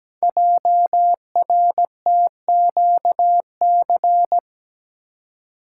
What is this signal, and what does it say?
Morse code, heard as a keyed tone, about 17 wpm, 700 Hz. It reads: JRTQC
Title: Code Groups - 5 Letters - 17wpm